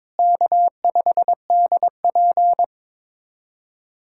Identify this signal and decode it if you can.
Morse code, heard as a keyed tone, about 21 wpm, 700 Hz. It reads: K5DP
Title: USA Callsign DE Callsign - 4 Characters - 22wpm